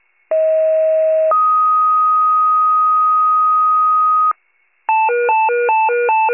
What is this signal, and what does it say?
Two-tone page, heard as a one-second tone, then a three-second tone, about 640 then 1250 Hz. It is heard from 0.3 to 4.3 s.